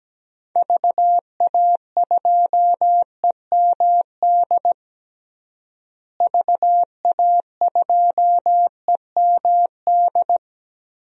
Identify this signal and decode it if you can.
Morse code, heard as a keyed tone, about 17 wpm, 700 Hz. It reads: VA2EMD VA2EMD
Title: DXCC-100 Call Signs - 6 Characters - Repeated 3x - 17wpm